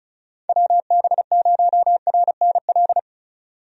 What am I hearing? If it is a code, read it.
Morse code, heard as a keyed tone, about 36 wpm, 700 Hz. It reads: WB0RNL
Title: USA Call Signs - 6 Characters - 35wpm